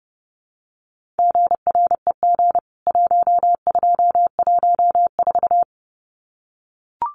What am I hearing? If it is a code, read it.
Morse code, heard as a keyed tone, about 30 wpm, 700 Hz. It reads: GREG 1214
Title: CWT Exchange 30wpm